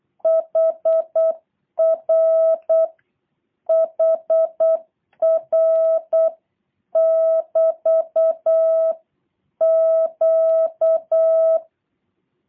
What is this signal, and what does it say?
Morse code, heard as a keyed tone, about 8 wpm, 650 Hz. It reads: HR HR <BT> Q
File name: jh8cbh / 2023cw / review1-6